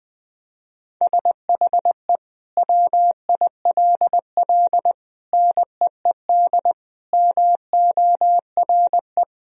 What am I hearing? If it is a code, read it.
Morse code, heard as a keyed tone, about 20 wpm, 700 Hz. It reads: SHE WILL NEED MORE
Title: Sentences from Top 1000 Words - Review Part 2 - 20wpm